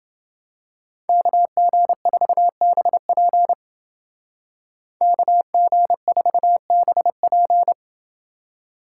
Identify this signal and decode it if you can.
Morse code, heard as a keyed tone, about 29 wpm, 700 Hz. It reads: KG4BP KG4BP